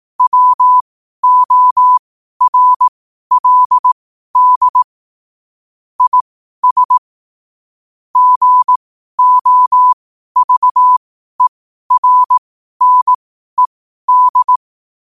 Morse code, heard as a keyed tone, about 18 wpm, spaced as for 12 wpm, 1000 Hz. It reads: WORLD IS GOVERNED
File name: qotdfiles / 12wpm